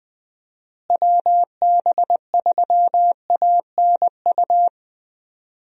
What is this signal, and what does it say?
Morse code, heard as a keyed tone, about 21 wpm, 700 Hz. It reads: WB3ANU
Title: USA Call Signs - 6 Characters - Repeated 2x - 20wpm